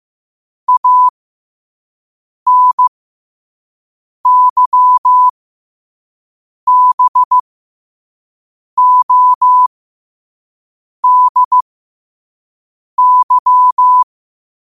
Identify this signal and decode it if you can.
Morse code, heard as a keyed tone, about 15 wpm, spaced as for 5 wpm, 1000 Hz. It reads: ANYBODY
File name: qotdfiles / 5wpm